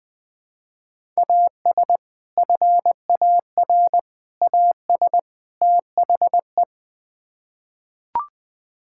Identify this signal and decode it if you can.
Morse code, heard as a keyed tone, about 20 wpm, 700 Hz. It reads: AS FAR AS THE
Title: Build A Phrase - 4-Word Phrases - 20wpm